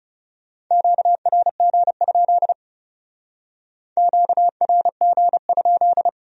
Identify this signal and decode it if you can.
Morse code, heard as a keyed tone, about 34 wpm, 700 Hz. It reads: QRG? QRG?
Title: Speed-Racing Q-Codes - 30wpm